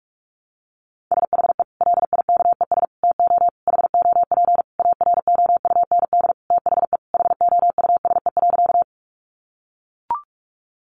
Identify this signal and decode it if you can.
Morse code, heard as a keyed tone, about 44 wpm, 700 Hz. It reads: SHE LIKES TO HOP AROUND THE HOUSE.
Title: Easy Phrases 45wpm